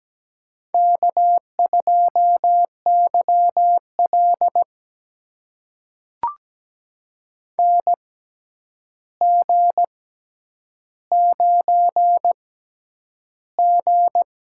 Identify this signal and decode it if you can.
Morse code, heard as a keyed tone, about 17 wpm, spaced as for 9 wpm, 700 Hz. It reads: K2YL NG9G
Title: USA Call Signs - 4 Characters - 6x Character Spacing - 17wpm